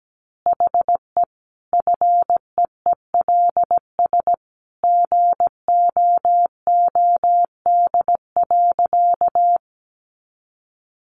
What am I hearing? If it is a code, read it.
Morse code, heard as a keyed tone, about 17 wpm, 700 Hz. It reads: HE FEELS GOOD.
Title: Additional Easy Phrases 17wpm